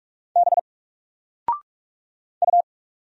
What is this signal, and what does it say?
Morse code, heard as a keyed tone, about 46 wpm, 700 Hz. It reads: B U
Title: Single Letter-Number 45wpm